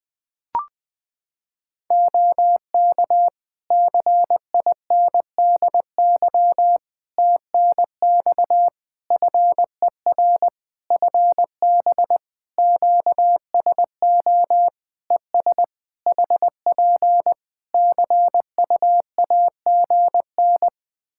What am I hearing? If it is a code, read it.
Morse code, heard as a keyed tone, about 20 wpm, 700 Hz. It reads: OK CINDY TNX FER FB QSO ES HP CUAGN
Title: K1USN SST QSO Practice 20wpm